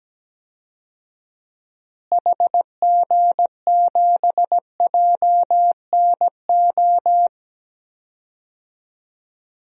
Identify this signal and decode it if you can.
Morse code, heard as a keyed tone, about 17 wpm, 700 Hz. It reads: HG7JNO
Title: DXCC-100 Call Signs - 6 Characters - Repeated 2x - 17wpm